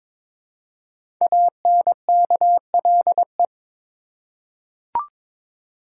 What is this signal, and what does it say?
Morse code, heard as a keyed tone, about 22 wpm, 700 Hz. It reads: ANKLE
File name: 5 Letter Words - First 3 Letters Given - 22wpm